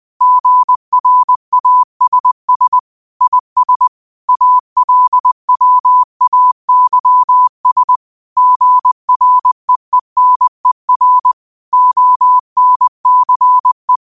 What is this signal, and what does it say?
Morse code, heard as a keyed tone, about 20 wpm, 1000 Hz. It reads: GRASS IS ALWAYS GREENER ONCE
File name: 20wpm